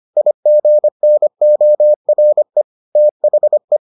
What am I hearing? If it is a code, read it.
Morse code, heard as a keyed tone, about 25 wpm, 600 Hz. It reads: IGNORE THE